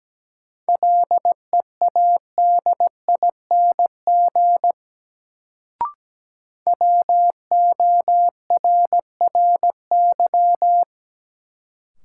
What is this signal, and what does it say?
Morse code, heard as a keyed tone, about 17 wpm, 700 Hz. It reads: LEADING WORRY
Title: Sentences from Top 1100 Words - New Word Drill - 17wpm